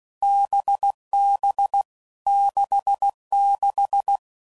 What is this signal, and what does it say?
Morse code, heard as a keyed tone, about 16 wpm, 800 Hz. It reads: BB 66
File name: Combinations / Difficult Combos / NDB NDB6